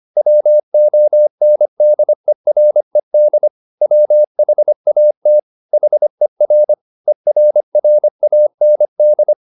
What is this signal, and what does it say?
Morse code, heard as a keyed tone, about 25 wpm, 600 Hz. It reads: WONDERED WHAT HER ERRAND